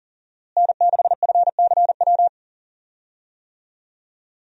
Morse code, heard as a keyed tone, about 37 wpm, 700 Hz. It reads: N6FCW
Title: USA Call Signs - 5 Characters - Repeated 3x - 40wpm